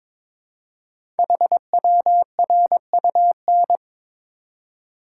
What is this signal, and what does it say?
Morse code, heard as a keyed tone, about 23 wpm, 700 Hz. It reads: HWRUN
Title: Code Groups - 5 Letters - 22wpm